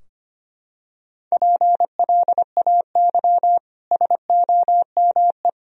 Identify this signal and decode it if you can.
Morse code, heard as a keyed tone, about 25 wpm, 700 Hz. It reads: PLAY SOME
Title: Sentences from Top 1000 Words - Review Part 3 - 25wpm